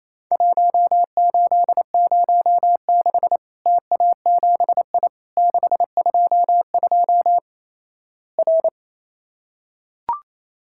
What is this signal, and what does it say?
Morse code, heard as a keyed tone, about 27 wpm, 700 Hz. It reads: 1806 TA7I 622 R
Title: Worked All Europe DX Contest - QTC Exchange - 28wpm